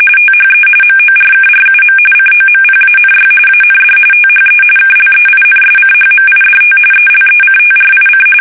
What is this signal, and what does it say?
RTTY, about 45.45 baud. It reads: THE MORKRUM COMPANY AND THE KLEINSCHMIDT ELECTRIC
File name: fsk850 / teletype-corp